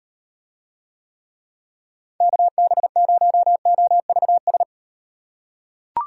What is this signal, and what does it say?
Morse code, heard as a keyed tone, about 38 wpm, 700 Hz. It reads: KB0OVS